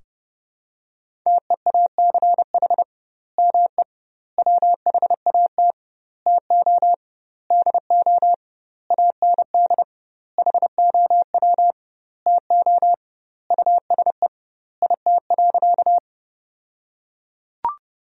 Morse code, heard as a keyed tone, about 30 wpm, 700 Hz. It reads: TEACH ME WHAT TO DO AND HOW TO USE IT.
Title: 2x Word-Spacing - Sentences from Top 500 Words - 30wpm